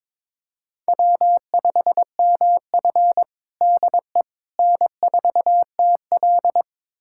Morse code, heard as a keyed tone, about 22 wpm, 700 Hz. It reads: W5MF DE N4TL
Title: USA Callsign DE Callsign - 4 Characters - 22wpm